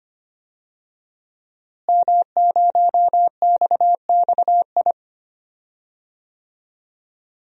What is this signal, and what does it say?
Morse code, heard as a keyed tone, about 25 wpm, 700 Hz. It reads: M0XXI
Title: DXCC-100 Call Signs - 5 Characters - Repeated 3x - 25wpm